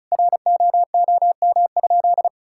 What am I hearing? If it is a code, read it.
Morse code, heard as a keyed tone, about 36 wpm, 700 Hz. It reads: ROOM?
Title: Sentences from Top 1000 Words - Review Part 5 - 35wpm